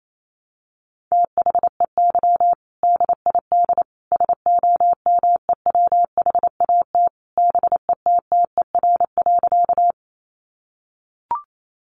Morse code, heard as a keyed tone, about 28 wpm, 700 Hz. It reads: THEY DID SOMEWHAT BETTER.
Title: Sentences from Top 1300 Words 28wpm